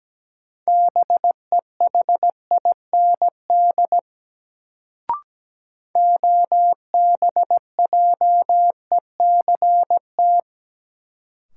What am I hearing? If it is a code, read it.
Morse code, heard as a keyed tone, about 17 wpm, 700 Hz. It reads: BEHIND OBJECT
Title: Sentences from Top 500 Words - New Word Drill - 17wpm